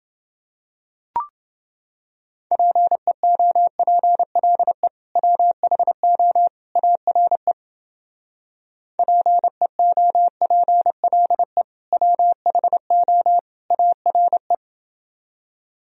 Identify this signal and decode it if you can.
Morse code, heard as a keyed tone, about 29 wpm, 700 Hz. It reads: PEOPLE WHO ARE PEOPLE WHO ARE